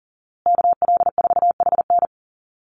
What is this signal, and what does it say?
Morse code, heard as a keyed tone, about 39 wpm, 700 Hz. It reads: KL4HN
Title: USA Call Signs Repeated 3x 40wpm